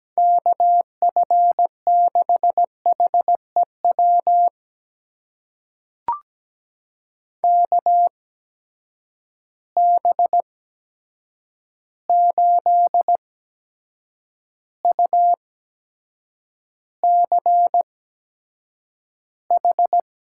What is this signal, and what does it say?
Morse code, heard as a keyed tone, about 17 wpm, spaced as for 7 wpm, 700 Hz. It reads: KF6HEW KB8UCH
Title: USA Call Signs - 6 Characters - 8x Character Spacing - 17wpm